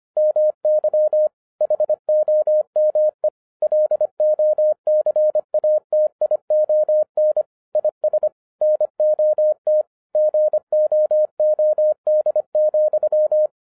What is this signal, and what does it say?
Morse code, heard as a keyed tone, about 25 wpm, 610 Hz. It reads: MY HOME LOCATION IS NOT GOOD,